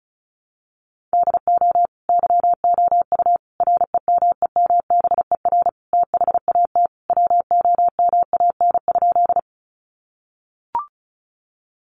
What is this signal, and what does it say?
Morse code, heard as a keyed tone, about 34 wpm, 700 Hz. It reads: DO YOU REMEMBER THAT WOMAN?
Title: Sentences from Top 1100 Words 35wpm